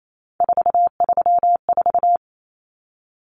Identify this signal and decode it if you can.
Morse code, heard as a keyed tone, about 28 wpm, 700 Hz. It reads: 434
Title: Common Cut Numbers - 3-Digit 28wpm